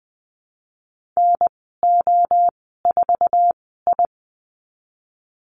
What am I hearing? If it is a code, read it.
Morse code, heard as a keyed tone, about 20 wpm, spaced as for 14 wpm, 700 Hz. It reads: NO4I
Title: 048 Lesson 9g - Number 4 - Call Signs 20wpm